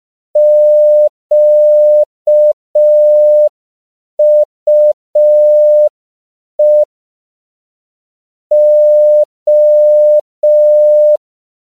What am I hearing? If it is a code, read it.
Morse code, heard as a keyed tone, about 5 wpm, 600 Hz. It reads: QUE O